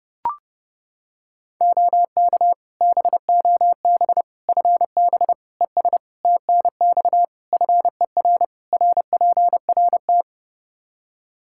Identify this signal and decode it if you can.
Morse code, heard as a keyed tone, about 30 wpm, 700 Hz. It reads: OK BOB FB ES TNX FER RPRT
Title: Practice QSO - Exchange 2 - Info - 30wpm